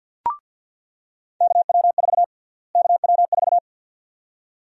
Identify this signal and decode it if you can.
Morse code, heard as a keyed tone, about 46 wpm, 700 Hz. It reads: KW4 KW4
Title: USA Call Sign Prefixes - Repeated 2x - 50wpm